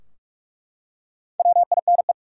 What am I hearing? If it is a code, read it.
Morse code, heard as a keyed tone, about 43 wpm, 700 Hz. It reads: WINE
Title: Four Letter Words - 45wpm